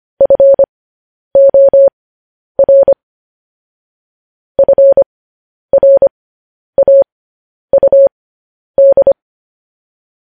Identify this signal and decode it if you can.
Morse code, heard as a keyed tone, about 25 wpm, spaced as for 10 wpm, 550 Hz. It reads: FOR FRAUD
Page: News Headlines in Morse Code at 10 WPM